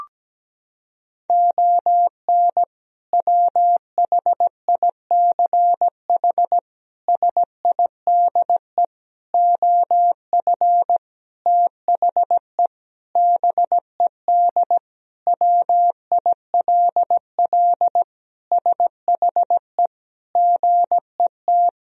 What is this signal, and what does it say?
Morse code, heard as a keyed tone, about 17 wpm, 700 Hz. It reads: ON WHICH SIDE OF THE BED WILL SHE GET
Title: Sentences from Top 1000 Words - Review Part 1 - 17wpm